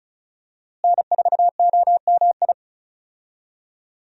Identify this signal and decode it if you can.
Morse code, heard as a keyed tone, about 36 wpm, 700 Hz. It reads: N4OMI
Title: USA Call Signs - 5 Characters - Repeated 2x - 35wpm